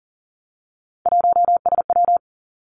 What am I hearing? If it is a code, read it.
Morse code, heard as a keyed tone, about 41 wpm, 700 Hz. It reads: 1SW
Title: Three Challenging Letter-Number Combinations 40wpm